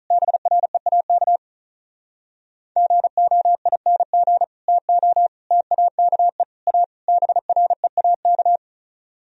Morse code, heard as a keyed tone, about 45 wpm, 700 Hz. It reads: BREAK GOING TO TAKE A BREAK